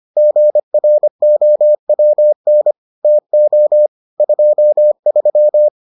Morse code, heard as a keyed tone, about 25 wpm, 600 Hz. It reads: GROWN TO 23